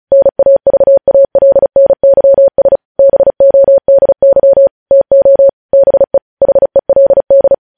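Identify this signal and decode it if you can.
Morse code, heard as a keyed tone, about 36 wpm, 550 Hz. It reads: NAVALNYS BODY TO BE HELD
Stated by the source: Morse code news headlines sent at 35 wpm.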